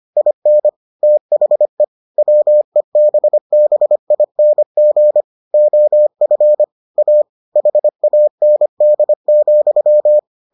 Morse code, heard as a keyed tone, about 25 wpm, 600 Hz. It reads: IN THE WEBBING OF A HAND,